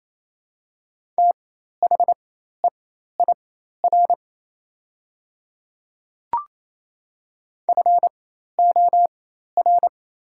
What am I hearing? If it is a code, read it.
Morse code, heard as a keyed tone, about 27 wpm, spaced as for 13 wpm, 690 Hz. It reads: THEIR FOR
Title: 4x Character Spacing - Top 200 Words - 28wpm